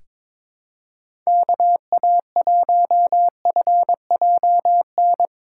Morse code, heard as a keyed tone, about 22 wpm, 700 Hz. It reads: KA1FJN
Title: USA Call Signs - 6 Characters - 2x Character Spacing - 22wpm